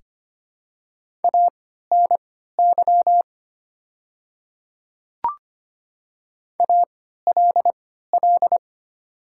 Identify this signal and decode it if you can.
Morse code, heard as a keyed tone, about 25 wpm, spaced as for 14 wpm, 690 Hz. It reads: ANY ALL
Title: 3x Character Spacing - Top 200 Words - 25wpm